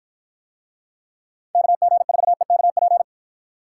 Morse code, heard as a keyed tone, about 52 wpm, 700 Hz. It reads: KG4EKP